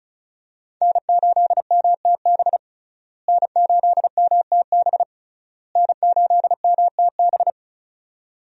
Morse code, heard as a keyed tone, about 36 wpm, 700 Hz. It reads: N8MTB N8MTB N8MTB
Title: USA Call Signs - 5 Characters - Repeated 3x - 35wpm